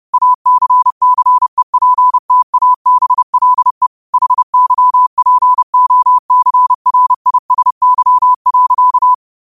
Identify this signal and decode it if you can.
Morse code, heard as a keyed tone, about 30 wpm, 1000 Hz. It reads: ACCEPTABLE HYPOCRISY.
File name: qotdfiles / 30wpm